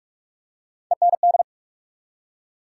Morse code, heard as a keyed tone, about 44 wpm, 700 Hz. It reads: END
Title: Three Letter Words - 45wpm